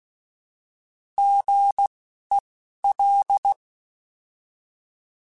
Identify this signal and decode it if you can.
Morse code, heard as a keyed tone, about 16 wpm, spaced as for 12 wpm, 800 Hz. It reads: GEL